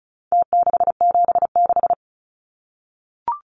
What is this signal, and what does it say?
Morse code, heard as a keyed tone, about 34 wpm, 700 Hz. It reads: T676
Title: Numbers - Contest Style - 4-Digit 35wpm